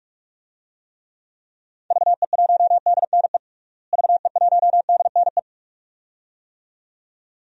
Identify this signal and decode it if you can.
Morse code, heard as a keyed tone, about 41 wpm, 700 Hz. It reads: VE1DNE VE1DNE
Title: DXCC-100 Call Signs - 6 Characters - Repeated 2x - 45wpm